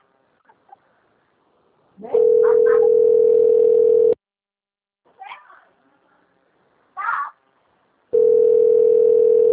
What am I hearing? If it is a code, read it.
Ringback tone (North America), 440 + 480 Hz.